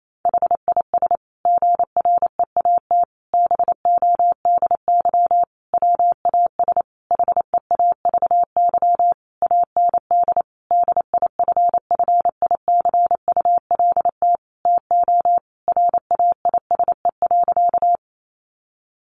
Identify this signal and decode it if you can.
Morse code, heard as a keyed tone, about 28 wpm, 700 Hz. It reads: HIS GREAT BODY WAS HEAVY AND DIFFICULT TO RAISE.
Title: Sentences from Top 1400 Words 28wpm